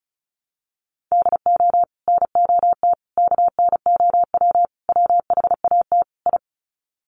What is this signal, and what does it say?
Morse code, heard as a keyed tone, about 34 wpm, 700 Hz. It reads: DO NOT KNOW WHAT I
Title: Sets of 5 Words 35wpm